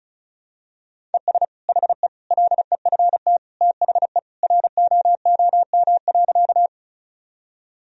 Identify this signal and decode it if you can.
Morse code, heard as a keyed tone, about 36 wpm, 700 Hz. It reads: ES HE LEFT THE ROOM.
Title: Sentences from Top 300 Words and Abbreviations 35wpm